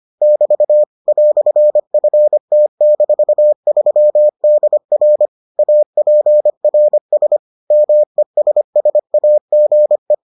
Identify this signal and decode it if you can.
Morse code, heard as a keyed tone, about 25 wpm, 600 Hz. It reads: <BT> "FT-3DR APRS MESSAGE